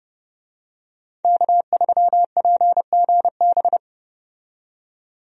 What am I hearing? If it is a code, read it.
Morse code, heard as a keyed tone, about 30 wpm, 700 Hz. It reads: K3PGB
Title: USA Call Signs - 5 Characters - 30wpm